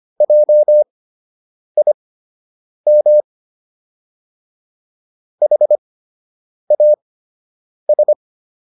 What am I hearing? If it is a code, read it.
Morse code, heard as a keyed tone, about 25 wpm, spaced as for 8 wpm, 600 Hz. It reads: JIM HAS